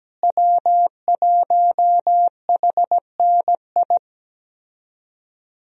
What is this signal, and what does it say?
Morse code, heard as a keyed tone, about 17 wpm, 700 Hz. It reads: W1HNI
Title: USA Call Signs - 5 Characters - Repeated 2x - 17wpm